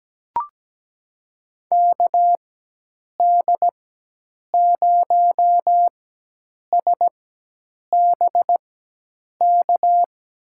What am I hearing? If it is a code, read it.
Morse code, heard as a keyed tone, about 17 wpm, spaced as for 8 wpm, 700 Hz. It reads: KD0SBK
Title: USA Call Signs - 4x Character Spacing - 17wpm